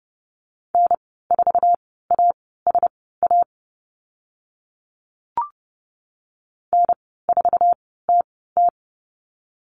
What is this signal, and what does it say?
Morse code, heard as a keyed tone, about 30 wpm, spaced as for 17 wpm, 700 Hz. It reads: N4ASA N4TT
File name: Lesson 14g - Number 2 - Call Signs 30wpm